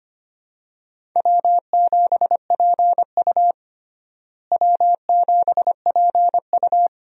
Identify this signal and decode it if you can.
Morse code, heard as a keyed tone, about 25 wpm, 700 Hz. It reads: W7PU W7PU
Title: USA Call Signs - 4 Characters - Repeated 2x - 25wpm